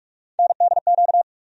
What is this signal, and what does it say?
Morse code, heard as a keyed tone, about 46 wpm, 700 Hz. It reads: NDQ
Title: USA Call Sign Postfixes - 3 Letters 45wpm